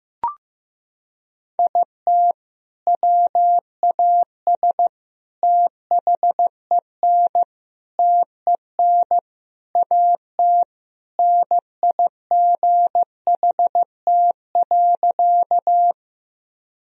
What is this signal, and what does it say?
Morse code, heard as a keyed tone, about 16 wpm, 700 Hz. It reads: IT WAS THEN TEN AT NIGHT.
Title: Sentences from Top 1000 Words - Review Part 2 - 15wpm